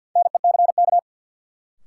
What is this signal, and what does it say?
Morse code, heard as a keyed tone, about 53 wpm, 700 Hz. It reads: NECK
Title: Sets of 5 Words Encore - New Word Drill - 50wpm